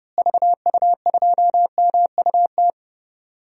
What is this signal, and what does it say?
Morse code, heard as a keyed tone, about 30 wpm, 700 Hz. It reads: VU2MUT
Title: DXCC-100 Call Signs - 6 Characters - Repeated 2x - 30wpm